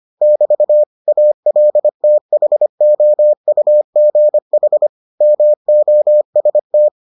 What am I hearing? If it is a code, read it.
Morse code, heard as a keyed tone, about 25 wpm, 600 Hz. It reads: <BT> ALTHOUGH MOST